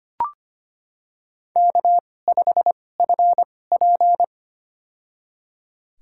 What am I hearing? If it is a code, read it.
Morse code, heard as a keyed tone, about 25 wpm, spaced as for 18 wpm, 700 Hz. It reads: K5FP
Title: USA Call Signs - 2x Character Spacing - 25wpm